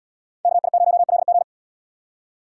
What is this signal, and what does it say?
Morse code, heard as a keyed tone, about 77 wpm, 700 Hz. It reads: KI0LG